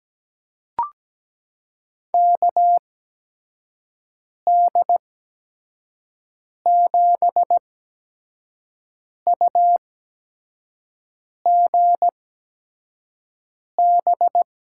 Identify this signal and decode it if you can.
Morse code, heard as a keyed tone, about 17 wpm, spaced as for 5 wpm, 700 Hz. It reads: KD7UGB
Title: USA Call Signs - 6 Characters - 8x Character Spacing - 17wpm